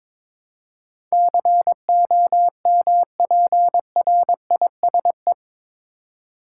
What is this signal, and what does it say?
Morse code, heard as a keyed tone, about 22 wpm, 700 Hz. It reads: COMPRISE
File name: practice / 22wpm / Eight Letter Words - 22wpm